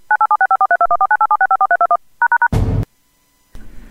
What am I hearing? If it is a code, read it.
DTMF, heard as telephone touch tones, tones of about 80 ms. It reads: 9876543214987654321#